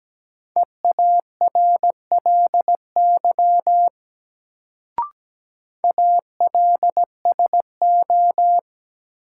Morse code, heard as a keyed tone, about 17 wpm, 700 Hz. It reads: EARLY ALSO
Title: Sentences from Top 400 Words - New Word Drill - 17wpm